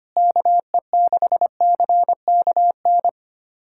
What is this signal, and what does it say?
Morse code, heard as a keyed tone, about 25 wpm, 700 Hz. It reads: KE6CKN
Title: USA Callsign DE Callsign 25wpm